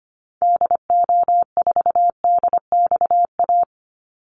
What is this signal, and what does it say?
Morse code, heard as a keyed tone, about 25 wpm, 700 Hz. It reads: DO4DXA
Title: Call Signs Repeated 3x - 25wpm